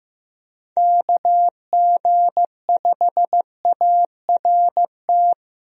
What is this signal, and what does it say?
Morse code, heard as a keyed tone, about 15 wpm, 700 Hz. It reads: KG5ART
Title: USA Call Signs - 6 Characters - Repeated 2x - 15wpm